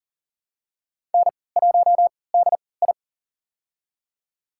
Morse code, heard as a keyed tone, about 40 wpm, spaced as for 23 wpm, 700 Hz. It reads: N1DI
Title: USA Call Signs - 4 Characters - 3x Character Spacing - 40wpm